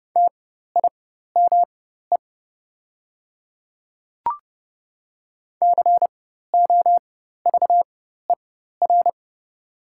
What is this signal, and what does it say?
Morse code, heard as a keyed tone, about 30 wpm, spaced as for 14 wpm, 700 Hz. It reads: TIME COVER
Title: 4x Character Spacing - Top 500 Words - 30wpm